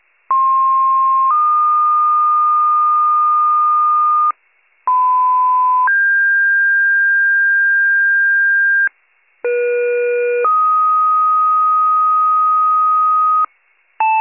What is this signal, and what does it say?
Two-tone page, heard as a one-second tone, then a three-second tone, about 1060 then 1220 Hz, about 1010 then 1600 Hz, about 480 then 1250 Hz. It is heard from 0.3 to 13.5 s.